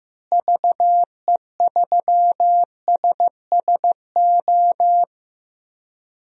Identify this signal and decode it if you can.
Morse code, heard as a keyed tone, about 15 wpm, 700 Hz. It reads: VE3SSO
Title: DXCC-100 Call Signs - 6 Characters - Repeated 3x - 15wpm